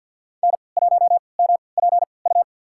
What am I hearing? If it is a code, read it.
Morse code, heard as a keyed tone, about 52 wpm, spaced as for 29 wpm, 690 Hz. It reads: N1MPU
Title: USA Call Signs - 5 Characters - 3x Character Spacing - 50wpm